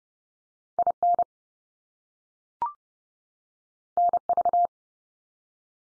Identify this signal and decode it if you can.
Morse code, heard as a keyed tone, about 30 wpm, 700 Hz. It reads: IN NV
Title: US States 30WPM